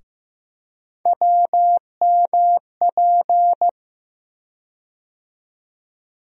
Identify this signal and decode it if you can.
Morse code, heard as a keyed tone, about 15 wpm, 700 Hz. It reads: WMP
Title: USA Call Sign Postfixes - 3 Letters 15wpm